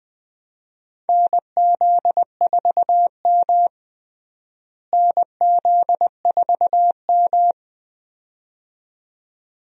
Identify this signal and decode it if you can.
Morse code, heard as a keyed tone, about 21 wpm, 700 Hz. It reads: NZ4M NZ4M
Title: USA Call Signs - 4 Characters - Repeated 2x - 20wpm